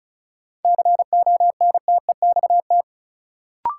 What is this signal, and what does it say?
Morse code, heard as a keyed tone, about 34 wpm, 700 Hz. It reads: CONTEXT
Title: Sets of 4 Words - New Word Drill - 35wpm